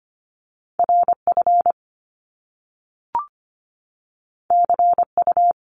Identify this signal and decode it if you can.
Morse code, heard as a keyed tone, about 25 wpm, 700 Hz. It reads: RF CU
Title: Top Two Letter Combinations in Words 25wpm